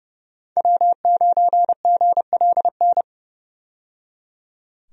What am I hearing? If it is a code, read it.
Morse code, heard as a keyed tone, about 30 wpm, 700 Hz. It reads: W9GLN
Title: USA Call Signs - 5 Characters - 30wpm